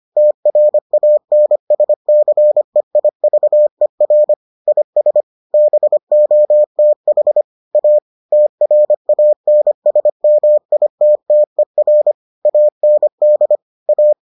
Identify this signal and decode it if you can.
Morse code, heard as a keyed tone, about 25 wpm, 600 Hz. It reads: TRANSCEIVER IS BOTH A TRANSMITTER AND A